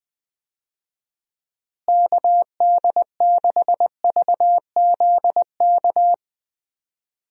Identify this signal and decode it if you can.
Morse code, heard as a keyed tone, about 21 wpm, 700 Hz. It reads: KD6VZK
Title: USA Call Signs - 6 Characters - Repeated 3x - 20wpm